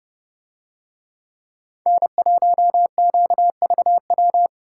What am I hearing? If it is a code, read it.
Morse code, heard as a keyed tone, about 30 wpm, 700 Hz. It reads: N1QVW